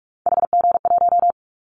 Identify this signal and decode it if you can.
Morse code, heard as a keyed tone, about 44 wpm, 700 Hz. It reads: HG1
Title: Callsign Prefixes for Top 25 DXCC - Repeated 3x - 45wpm